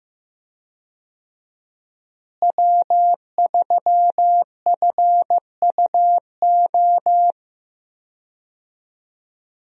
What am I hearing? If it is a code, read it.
Morse code, heard as a keyed tone, about 15 wpm, 700 Hz. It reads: W3FUO
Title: USA Call Signs - 5 Characters - Repeated 3x - 15wpm